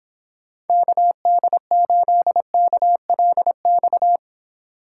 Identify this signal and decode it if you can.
Morse code, heard as a keyed tone, about 30 wpm, 700 Hz. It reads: KD8KLX